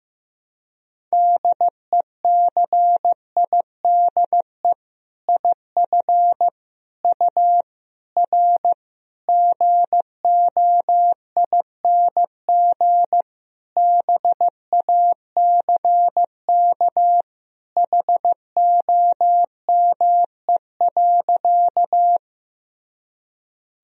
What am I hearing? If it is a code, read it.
Morse code, heard as a keyed tone, about 15 wpm, 700 Hz. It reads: DECIDE IF U R GOING BACK HOME.
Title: Sentences from Top 500 Words and Abbreviations 15wpm